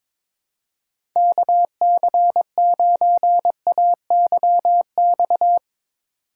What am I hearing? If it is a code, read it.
Morse code, heard as a keyed tone, about 22 wpm, 700 Hz. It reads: KC9AYX
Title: USA Call Signs - 6 Characters - 3x Character Spacing - 22wpm